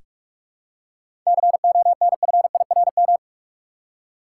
Morse code, heard as a keyed tone, about 46 wpm, 700 Hz. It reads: CONFIRM
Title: Seven Letter Words - 45wpm